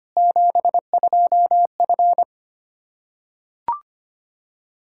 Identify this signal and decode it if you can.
Morse code, heard as a keyed tone, about 25 wpm, 700 Hz. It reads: 72F
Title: Practice QSO - Exchange 2 - Info - 25wpm